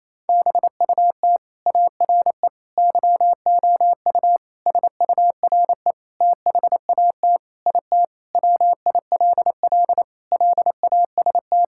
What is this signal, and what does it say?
Morse code, heard as a keyed tone, about 27 wpm, 700 Hz. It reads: BUT ARE YOU SURE THAT IT WILL LAST
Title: Sentences from Top 1600 Words 28wpm